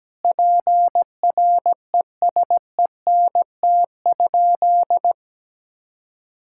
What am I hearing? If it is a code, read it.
Morse code, heard as a keyed tone, about 17 wpm, 700 Hz. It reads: PRESENT?
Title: Sentences from Top 1000 Words - Review Part 6 - 17wpm